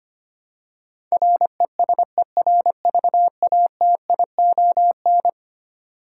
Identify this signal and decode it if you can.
Morse code, heard as a keyed tone, about 25 wpm, 700 Hz. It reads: RESERVATION
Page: morse-code-ninja-practice-25wpm